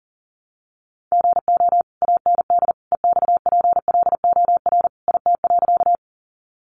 Morse code, heard as a keyed tone, about 40 wpm, 700 Hz. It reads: GO AND EXPLOR IT.
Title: Sentences from Top 1100 Words 40wpm